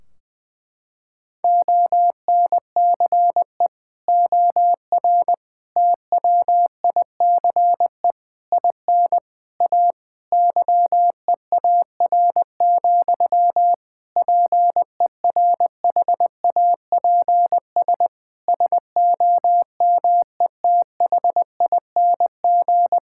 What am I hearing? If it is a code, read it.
Morse code, heard as a keyed tone, about 20 wpm, 700 Hz. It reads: ONCE OR TWICE IN A YEAR, PERHAPS SOMETHING
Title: Sentences from Top 1600 Words 20wpm